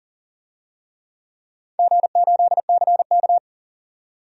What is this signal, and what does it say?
Morse code, heard as a keyed tone, about 38 wpm, 700 Hz. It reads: G8CK
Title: DXCC-100 Call Signs - 4 Characters - 40wpm